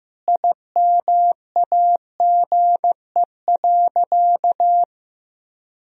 Morse code, heard as a keyed tone, about 15 wpm, 700 Hz. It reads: IMAGE.
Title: Sentences from Top 1000 Words - Review Part 4 - 15wpm